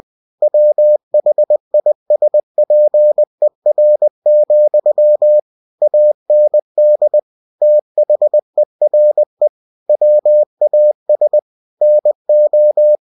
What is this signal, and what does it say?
Morse code, heard as a keyed tone, about 20 wpm, 600 Hz. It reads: WHISPER, AND THERE WAS NO
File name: cw-books / winnie-the-pooh / winnie-20wpm-page-0082